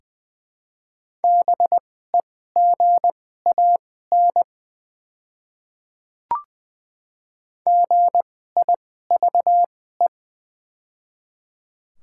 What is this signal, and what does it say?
Morse code, heard as a keyed tone, about 20 wpm, spaced as for 15 wpm, 700 Hz. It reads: BEGAN GIVE
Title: 2x Character Spacing - Top 300 Words - 20wpm